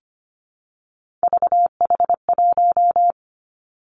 Morse code, heard as a keyed tone, about 25 wpm, 700 Hz. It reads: VH1
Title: Three Challenging Letter-Number Combinations 25wpm